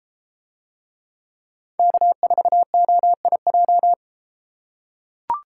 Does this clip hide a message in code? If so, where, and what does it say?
Yes: Morse code from 1.8 to 3.9 s, heard as a keyed tone, about 33 wpm, 700 Hz. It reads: K4OIJ